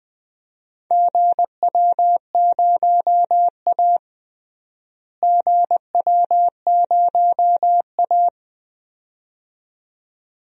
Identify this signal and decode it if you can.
Morse code, heard as a keyed tone, about 20 wpm, 700 Hz. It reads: GW0A GW0A
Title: DXCC-100 Call Signs - 4 Characters - Repeated 2x - 20wpm